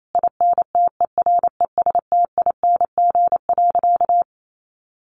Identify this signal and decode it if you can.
Morse code, heard as a keyed tone, about 28 wpm, 700 Hz. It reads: INTERESTING.
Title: Sentences from Top 1000 Words 28wpm